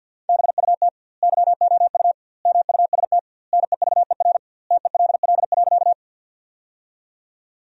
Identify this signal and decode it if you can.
Morse code, heard as a keyed tone, about 50 wpm, 700 Hz. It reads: BUT YOU MUST NEVER TELL.
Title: 2x Word-Spacing - Sentences from Top 300 Words - 50wpm